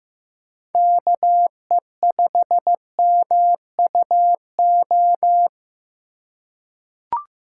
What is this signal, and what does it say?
Morse code, heard as a keyed tone, about 15 wpm, 700 Hz. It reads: KE5MUO
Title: USA Call Signs - 6 Characters - 4x Character Spacing - 15wpm